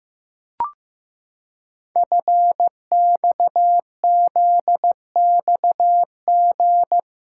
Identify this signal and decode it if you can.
Morse code, heard as a keyed tone, about 15 wpm, 700 Hz. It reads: FXZXG
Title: Code Groups - 5 Letters - 15wpm